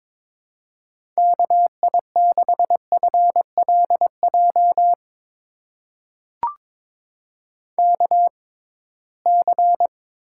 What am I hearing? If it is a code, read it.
Morse code, heard as a keyed tone, about 23 wpm, spaced as for 17 wpm, 700 Hz. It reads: KI6FLJ KC
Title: USA Call Signs - 6 Characters - 6x Character Spacing - 22wpm